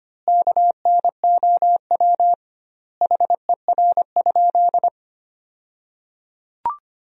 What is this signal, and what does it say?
Morse code, heard as a keyed tone, about 25 wpm, 700 Hz. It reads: KNOW HER?
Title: 2x Word-Spacing - Sentences from Top 100 Words - 25wpm